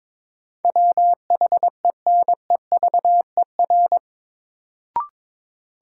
Morse code, heard as a keyed tone, about 21 wpm, 700 Hz. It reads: WHENEVER
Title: Sentences from Top 1200 Words - New Word Drill - 22wpm